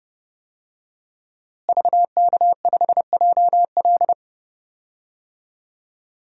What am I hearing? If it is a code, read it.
Morse code, heard as a keyed tone, about 30 wpm, 700 Hz. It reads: VK5JL
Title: DXCC-100 Call Signs - 5 Characters - Repeated 3x - 30wpm